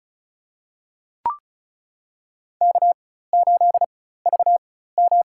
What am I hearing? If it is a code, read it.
Morse code, heard as a keyed tone, about 34 wpm, spaced as for 16 wpm, 700 Hz. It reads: K8VM
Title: USA Call Signs - 4 Characters - 4x Character Spacing - 35wpm